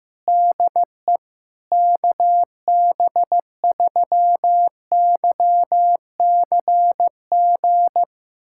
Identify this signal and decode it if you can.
Morse code, heard as a keyed tone, about 15 wpm, 700 Hz. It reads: DE KB3YCG
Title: USA Callsign DE Callsign - 6 Characters - 15wpm